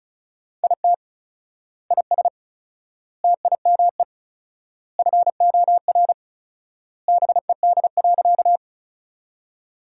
Morse code, heard as a keyed tone, about 34 wpm, 700 Hz. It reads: IT IS TIME FOR BED.
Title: 4x Word-Spacing - Sentences from Top 500 Words - 35wpm